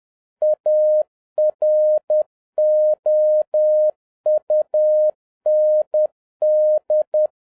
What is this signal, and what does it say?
Morse code, heard as a keyed tone, about 10 wpm, 610 Hz. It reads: AROUND